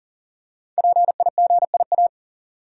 Morse code, heard as a keyed tone, about 41 wpm, 700 Hz. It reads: PIGIA
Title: Code Groups - 5 Letters - 40wpm